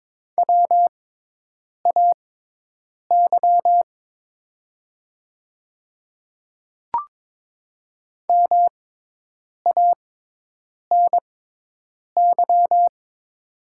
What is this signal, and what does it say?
Morse code, heard as a keyed tone, about 22 wpm, spaced as for 8 wpm, 700 Hz. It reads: WAY MANY
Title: 6x Character Spacing - Top 100 Words - 22wpm